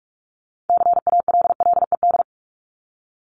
Morse code, heard as a keyed tone, about 44 wpm, 700 Hz. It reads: CALLED
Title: ED Suffix - Top 100 Words with Base Words 45wpm